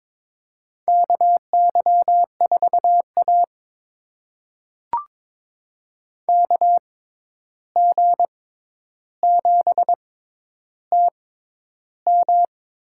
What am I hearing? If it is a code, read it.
Morse code, heard as a keyed tone, about 23 wpm, spaced as for 11 wpm, 700 Hz. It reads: KY4A KG7TM
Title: USA Call Signs - 6x Character Spacing - 22wpm